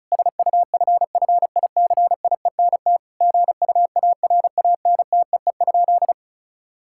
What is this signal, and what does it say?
Morse code, heard as a keyed tone, about 36 wpm, 700 Hz. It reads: SUFFICIENT GUARANTEE?
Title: Sentences from Top 1600 Words 35wpm